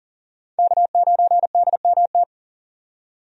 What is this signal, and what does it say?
Morse code, heard as a keyed tone, about 40 wpm, 700 Hz. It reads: K9DMT
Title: USA Call Signs - 5 Characters - 40wpm